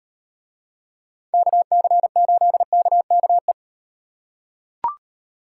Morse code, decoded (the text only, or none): KC8KKE